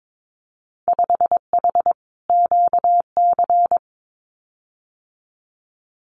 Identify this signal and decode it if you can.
Morse code, heard as a keyed tone, about 22 wpm, 700 Hz. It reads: 5H QC
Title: Winter Field Day Exchanges Encore 22wpm